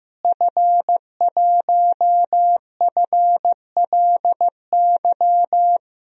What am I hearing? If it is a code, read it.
Morse code, heard as a keyed tone, about 16 wpm, 700 Hz. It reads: F1FLY
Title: DXCC-100 Call Signs - 5 Characters - Repeated 3x - 15wpm